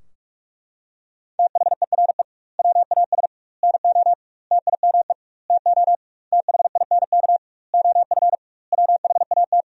Morse code, heard as a keyed tone, about 41 wpm, 700 Hz. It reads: THERE WAS NO TIME TO THINK OF WHAT
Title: 2x Word-Spacing - Sentences from Top 400 Words - 45wpm